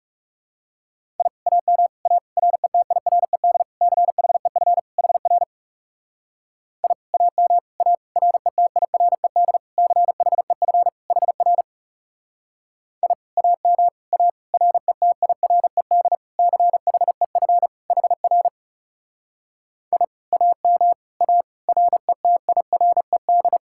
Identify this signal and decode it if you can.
Morse code, heard as a keyed tone, about 37 wpm, 700 Hz. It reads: I AM A RETIRED CHEF HR I AM A RETIRED CHEF HR I AM A RETIRED CHEF HR I AM A RETIRED
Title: Speed-Racing Common QSO Phrases Encore - 30wpm